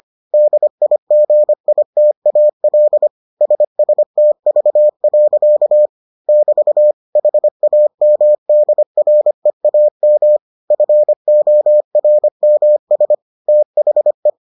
Morse code, heard as a keyed tone, about 25 wpm, 600 Hz. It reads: DIGITAL SSTV. <BT> HAMDREAM FORMS THE